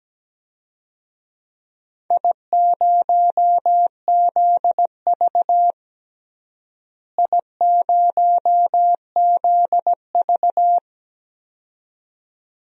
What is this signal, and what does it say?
Morse code, heard as a keyed tone, about 17 wpm, 700 Hz. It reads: I0ZV I0ZV
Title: DXCC-100 Call Signs - 4 Characters - Repeated 2x - 17wpm